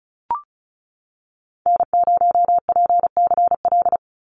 Morse code, heard as a keyed tone, about 36 wpm, 700 Hz. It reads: N0PCL
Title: NA SOTA Activator Callsigns 35wpm